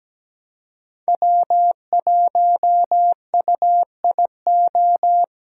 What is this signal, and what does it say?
Morse code, heard as a keyed tone, about 17 wpm, 700 Hz. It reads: W1UIO
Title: USA Callsign DE Callsign - 5 Characters - 17wpm